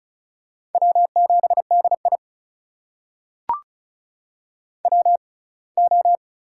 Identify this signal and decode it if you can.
Morse code, heard as a keyed tone, about 36 wpm, spaced as for 24 wpm, 700 Hz. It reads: W7DI WO
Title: USA Call Signs - 4 Characters - 6x Character Spacing - 35wpm